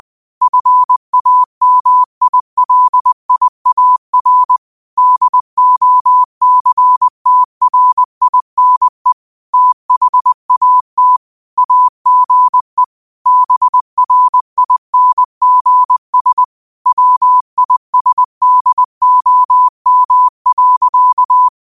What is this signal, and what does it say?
Morse code, heard as a keyed tone, about 20 wpm, 1000 Hz. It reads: FAMILIAR DOCTRINE THAT AGE BRINGS WISDOM.
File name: qotdfiles / 20wpm